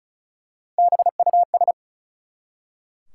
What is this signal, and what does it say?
Morse code, heard as a keyed tone, about 36 wpm, 700 Hz. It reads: BUS
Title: Three Letter Words - 35wpm